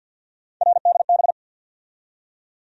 Morse code, heard as a keyed tone, about 54 wpm, 700 Hz. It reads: RDB
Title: USA Call Sign Postfixes - 3 Letters 50wpm